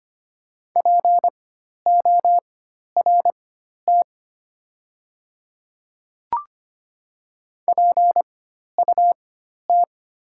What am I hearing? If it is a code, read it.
Morse code, heard as a keyed tone, about 25 wpm, spaced as for 12 wpm, 700 Hz. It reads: PORT PUT
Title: 4x Character Spacing - Top 200 Words - 25wpm